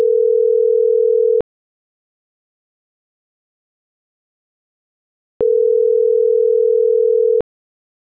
Ringback tone (North America), 440 + 480 Hz.